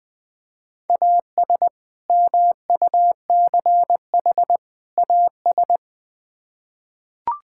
Morse code, heard as a keyed tone, about 21 wpm, 700 Hz. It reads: AS MUCH AS
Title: Build A Phrase - 3-Word Phrases - 20wpm